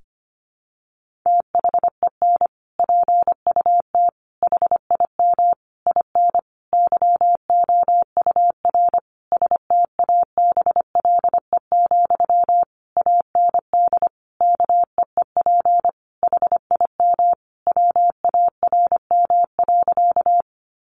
Morse code, heard as a keyed tone, about 25 wpm, 700 Hz. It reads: THEN PUT HIM IN YOUR STABLE, AND KEEP HIM WARM.
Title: Sentences from Top 1500 Words 25wpm